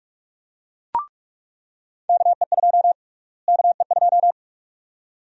Morse code, heard as a keyed tone, about 47 wpm, 700 Hz. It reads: KE2 KE2
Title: USA Call Sign Prefixes - Repeated 2x - 45wpm